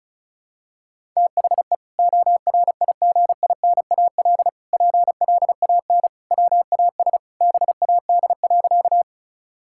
Morse code, heard as a keyed tone, about 36 wpm, 700 Hz. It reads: THE ORIGINAL PLAN WAS BAD.
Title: Sentences from Top 1000 Words - Review Part 6 - 35wpm